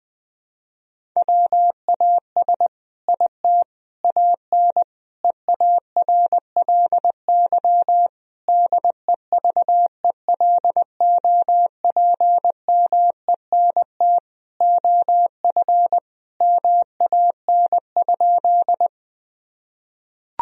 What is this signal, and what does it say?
Morse code, heard as a keyed tone, about 21 wpm, 700 Hz. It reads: WAS IT AN EARLY DEVELOPMENT OF MAN?
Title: Sentences from Top 1000 Words - Review Part 4 - 20wpm